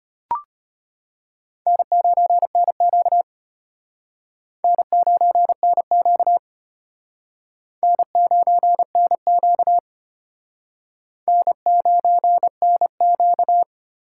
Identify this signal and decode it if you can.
Morse code, heard as a keyed tone, about 33 wpm, 700 Hz. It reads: N9NQ N9NQ N9NQ N9NQ